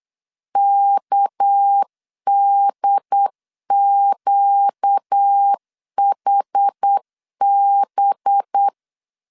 Morse code, heard as a keyed tone, about 8 wpm, 800 Hz. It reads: KDQHB
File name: CW / DET / 1 6 1500C S B 8wpm 5142